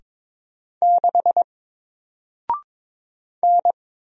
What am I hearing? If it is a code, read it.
Morse code, heard as a keyed tone, about 22 wpm, 700 Hz. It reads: 6 N
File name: Single-Letter-Number-22wpm